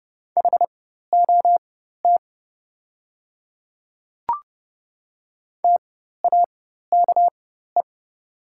Morse code, heard as a keyed tone, about 30 wpm, spaced as for 14 wpm, 700 Hz. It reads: HOT TAKE
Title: 4x Character Spacing - Top 500 Words - 30wpm